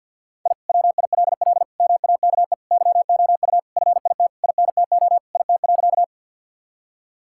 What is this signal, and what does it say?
Morse code, heard as a keyed tone, about 50 wpm, 700 Hz. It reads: I WILL MAKE YOU FIT INTO IT.
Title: Sentences from Top 1000 Words - Review Part 5 - 50wpm